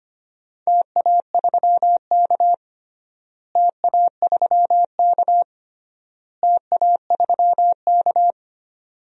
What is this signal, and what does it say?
Morse code, heard as a keyed tone, about 25 wpm, 700 Hz. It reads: TA3K TA3K TA3K
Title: DXCC-100 Call Signs - 4 Characters - Repeated 3x - 25wpm